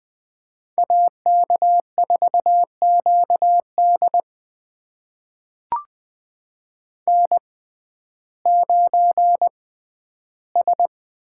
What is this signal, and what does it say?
Morse code, heard as a keyed tone, about 20 wpm, spaced as for 12 wpm, 700 Hz. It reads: AK4QD N9S
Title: USA Call Signs - 5 Characters - 6x Character Spacing - 20wpm